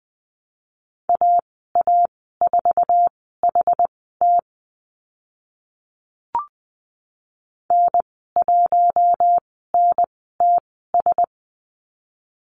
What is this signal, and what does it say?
Morse code, heard as a keyed tone, about 20 wpm, spaced as for 15 wpm, 700 Hz. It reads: AA4HT N1NTS
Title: 062 Lesson 11g - Letter h - Call Signs 20wpm